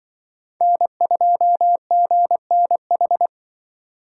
Morse code, heard as a keyed tone, about 24 wpm, 700 Hz. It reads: N2GNH
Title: Speed Racing - USA 1x3 Call Signs 20wpm